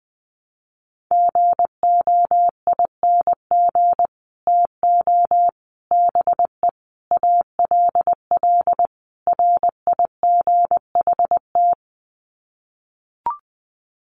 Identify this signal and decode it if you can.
Morse code, heard as a keyed tone, about 20 wpm, 700 Hz. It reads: GOING TO BE ALL RIGHT
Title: Sets of 5 Words Encore2 20wpm